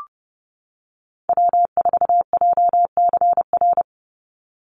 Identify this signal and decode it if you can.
Morse code, heard as a keyed tone, about 30 wpm, 700 Hz. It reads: W4JCR
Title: USA Call Signs 30wpm